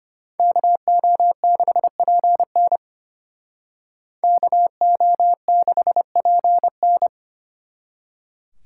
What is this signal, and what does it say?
Morse code, heard as a keyed tone, about 30 wpm, 700 Hz. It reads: KO6PN KO6PN
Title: Speed Racing - USA 2x2 Call Signs 25wpm